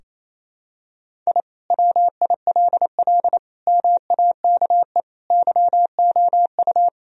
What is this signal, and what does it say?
Morse code, heard as a keyed tone, about 28 wpm, 700 Hz. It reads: I WILL MAKE YOU
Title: Sentences from Top 1000 Words - Review Part 4 - 28wpm